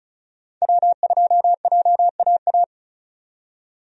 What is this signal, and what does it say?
Morse code, heard as a keyed tone, about 34 wpm, 700 Hz. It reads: W2JAA
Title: USA Call Signs - 5 Characters - 35wpm